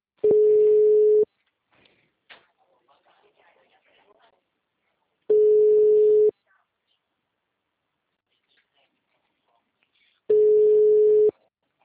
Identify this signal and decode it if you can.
Ringback tone (Europe, 425 Hz), with speech.